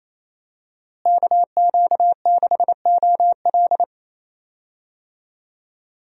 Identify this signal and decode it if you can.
Morse code, heard as a keyed tone, about 28 wpm, 700 Hz. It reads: KQ6OL
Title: USA Call Signs - 5 Characters - Repeated 3x - 28wpm